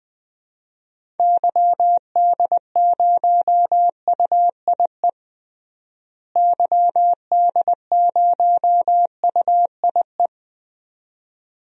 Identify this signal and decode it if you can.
Morse code, heard as a keyed tone, about 21 wpm, 700 Hz. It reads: YD0UIE YD0UIE
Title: DXCC-100 Call Signs - 6 Characters - Repeated 2x - 20wpm